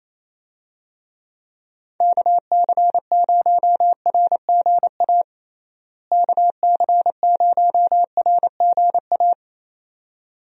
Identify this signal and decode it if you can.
Morse code, heard as a keyed tone, about 28 wpm, 700 Hz. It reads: KC0RGA KC0RGA
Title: USA Call Signs - 6 Characters - Repeated 2x - 28wpm